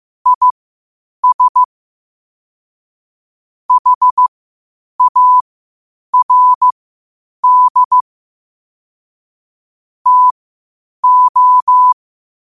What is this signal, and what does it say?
Morse code, heard as a keyed tone, about 15 wpm, spaced as for 8 wpm, 1000 Hz. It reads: IS HARD TO
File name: qotdfiles / 8wpm